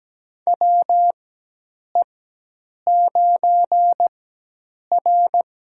Morse code, heard as a keyed tone, about 17 wpm, spaced as for 8 wpm, 700 Hz. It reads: WE9R
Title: USA Call Signs - 4 Characters - 4x Character Spacing - 17wpm